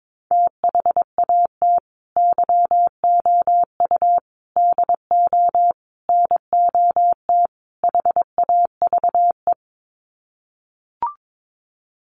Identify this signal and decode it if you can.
Morse code, heard as a keyed tone, about 22 wpm, 700 Hz. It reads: THAT YOU DO NOT HAVE
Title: Sets of 5 Words Encore 22wpm